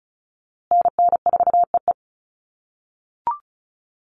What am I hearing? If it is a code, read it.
Morse code, heard as a keyed tone, about 34 wpm, 700 Hz. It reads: NN4EE
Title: USA Call Signs Repeated 2x 35wpm